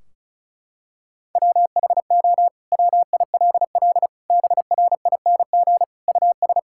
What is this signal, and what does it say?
Morse code, heard as a keyed tone, about 36 wpm, 700 Hz. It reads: WHO WILL BRING US
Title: Sentences from Top 1600 Words 35wpm